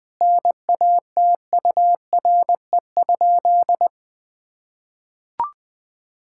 Morse code, heard as a keyed tone, about 20 wpm, 700 Hz. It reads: NATURE?
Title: Sentences from Top 1000 Words - Review Part 6 - 20wpm